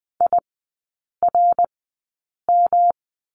Morse code, heard as a keyed tone, about 20 wpm, spaced as for 8 wpm, 700 Hz. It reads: IRM
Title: 107 Lesson 18e - Letter m - Three Characters 20wpm